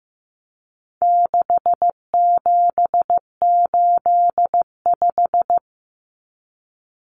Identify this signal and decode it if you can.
Morse code, heard as a keyed tone, about 15 wpm, 700 Hz. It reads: 6785
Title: Common Cut Numbers - 4-Digit 15wpm